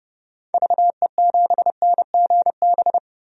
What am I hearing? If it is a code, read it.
Morse code, heard as a keyed tone, about 30 wpm, 700 Hz. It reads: VE7NGB
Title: DXCC-100 Call Signs - 6 Characters - Repeated 2x - 30wpm